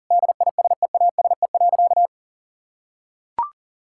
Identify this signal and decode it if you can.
Morse code, heard as a keyed tone, about 37 wpm, 700 Hz. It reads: DISEASE.
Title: Sentences from Top 1000 Words - Review Part 6 - 40wpm